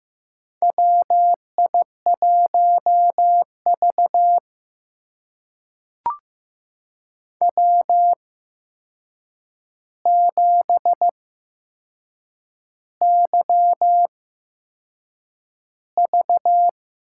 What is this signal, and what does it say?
Morse code, heard as a keyed tone, about 15 wpm, spaced as for 6 wpm, 700 Hz. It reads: WI1V W7YV
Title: USA Call Signs - 4 Characters - 8x Character Spacing - 15wpm